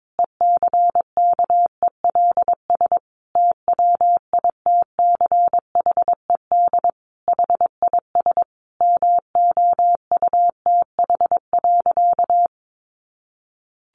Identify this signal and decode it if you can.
Morse code, heard as a keyed tone, about 22 wpm, 700 Hz. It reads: ECKELS TWITCHED HIS MOUTH.
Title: A Sound Of Thunder 22wpm